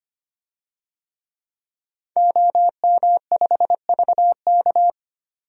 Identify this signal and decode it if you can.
Morse code, heard as a keyed tone, about 25 wpm, 700 Hz. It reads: OM5VK
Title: DXCC-100 Call Signs - 5 Characters - 25wpm